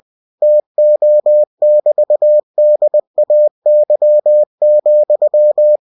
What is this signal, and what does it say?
Morse code, heard as a keyed tone, about 20 wpm, 600 Hz. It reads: TO<BT>DAY,
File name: cw-books / winnie-the-pooh / winnie-20wpm-page-0033